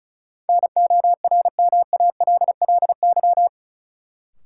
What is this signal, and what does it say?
Morse code, heard as a keyed tone, about 34 wpm, 700 Hz. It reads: NORMALLY
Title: Sentences from Top 1000 Words - Review Part 6 - New Word Drill - 35wpm